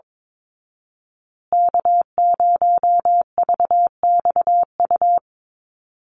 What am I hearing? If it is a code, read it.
Morse code, heard as a keyed tone, about 22 wpm, 700 Hz. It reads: K0VXU
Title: Call Signs Repeated 3x - 22wpm